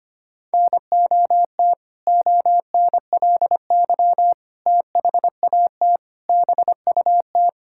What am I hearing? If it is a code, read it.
Morse code, heard as a keyed tone, about 25 wpm, 700 Hz. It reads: NOT ONLY THAT BUT
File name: practice / 25wpm / Sentences with 2-Word Phrases - 25wpm